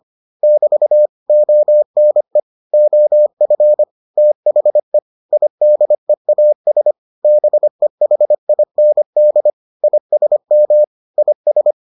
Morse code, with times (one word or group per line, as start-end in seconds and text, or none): 0.4-1.1 <BT>
1.3-2.4 ONE
2.7-3.8 OF
4.2-5.0 THE
5.3-6.9 IDEAS
7.2-9.5 BEHIND
9.8-10.8 ISM
11.2-11.7 IS